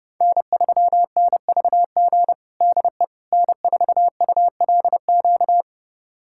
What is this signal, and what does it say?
Morse code, heard as a keyed tone, about 30 wpm, 700 Hz. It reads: N3NVG DE N4ULQ
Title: USA Callsign DE Callsign - 5 Characters - 30wpm